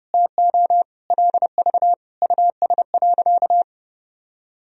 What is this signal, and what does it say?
Morse code, heard as a keyed tone, about 30 wpm, 700 Hz. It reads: TO LV US.
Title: Sentences from Top 400 Words and Abbreviations 30wpm